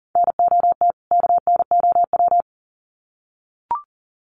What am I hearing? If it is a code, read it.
Morse code, heard as a keyed tone, about 40 wpm, 700 Hz. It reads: NOT KNOW
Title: Sets of 2 Words 40wpm